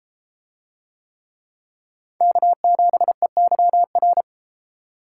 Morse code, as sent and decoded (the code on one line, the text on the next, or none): -.- --... -.-- .-.
K7EYR